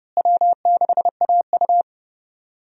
Morse code, heard as a keyed tone, about 29 wpm, 700 Hz. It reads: W6AU
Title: USA Call Signs - 4 Characters - Repeated 2x - 30wpm